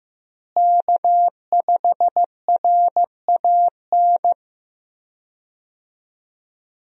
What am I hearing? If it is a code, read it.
Morse code, heard as a keyed tone, about 15 wpm, 700 Hz. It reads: K5RAN
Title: USA Call Signs - 5 Characters - Repeated 3x - 15wpm